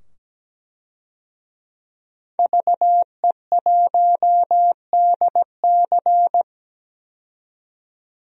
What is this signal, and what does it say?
Morse code, heard as a keyed tone, about 17 wpm, 700 Hz. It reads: VE1DC
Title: DXCC-100 Call Signs - 5 Characters - 17wpm